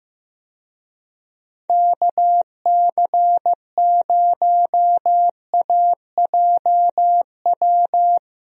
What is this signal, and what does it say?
Morse code, heard as a keyed tone, about 15 wpm, 700 Hz. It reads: KC0AJW
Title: USA Call Signs - 6 Characters - Repeated 3x - 15wpm